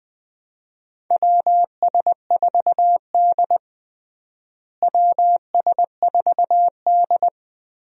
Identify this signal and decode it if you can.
Morse code, heard as a keyed tone, about 20 wpm, 700 Hz. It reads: WS4D WS4D
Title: USA Call Signs - 4 Characters - Repeated 2x - 20wpm